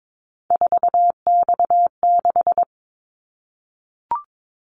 Morse code, heard as a keyed tone, about 22 wpm, 700 Hz. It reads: 4X6
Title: Callsign Prefixes for Top 50 DXCC - 22wpm